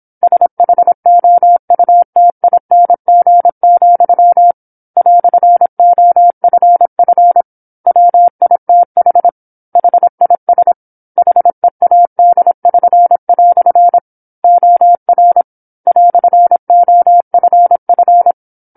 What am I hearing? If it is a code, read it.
Morse code, heard as a keyed tone, about 27 wpm, 700 Hz. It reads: SHOUTING, "OFF WITH HIS HEAD<SN>" OR "OFF
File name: 08-Chapter-08 26 WPM